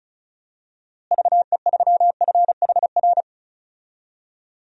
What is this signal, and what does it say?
Morse code, heard as a keyed tone, about 36 wpm, 700 Hz. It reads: VE3FHR
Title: DXCC-100 Call Signs - 6 Characters - Repeated 2x - 35wpm